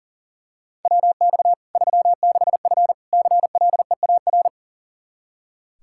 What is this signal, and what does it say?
Morse code, heard as a keyed tone, about 40 wpm, 700 Hz. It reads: WX 36F CLEAR
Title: Practice QSO - Exchange 2 - Info - 40wpm